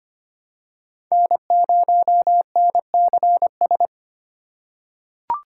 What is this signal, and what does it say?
Morse code, heard as a keyed tone, about 25 wpm, 700 Hz. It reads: N0NCS
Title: USA Call Signs - 5 Characters - 4x Character Spacing - 25wpm